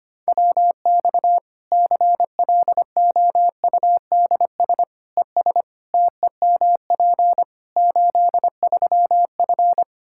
Morse code, heard as a keyed tone, about 25 wpm, 700 Hz. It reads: WX CLOUDS ES TEMP 83F
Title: K1USN SST QSO Practice 25wpm